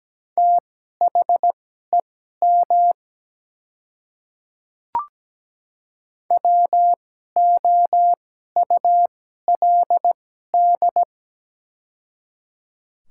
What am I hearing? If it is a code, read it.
Morse code, heard as a keyed tone, about 17 wpm, spaced as for 12 wpm, 700 Hz. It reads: THEM WOULD
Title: 2x Character Spacing - Top 200 Words - 17wpm